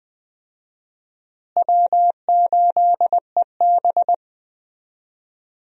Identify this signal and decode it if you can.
Morse code, heard as a keyed tone, about 20 wpm, 700 Hz. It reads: W8EB
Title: USA Call Signs - 4 Characters - Repeated 2x - 20wpm